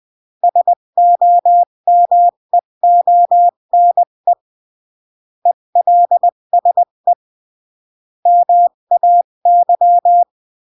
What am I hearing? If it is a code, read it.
Morse code, heard as a keyed tone, about 20 wpm, 700 Hz. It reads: SOMEONE ELSE MAY
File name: CW / jodorowsky / 82 commandments 20-15